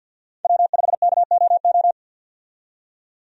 Morse code, heard as a keyed tone, about 50 wpm, 700 Hz. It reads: W5KOO
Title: USA Call Signs - 5 Characters - 50wpm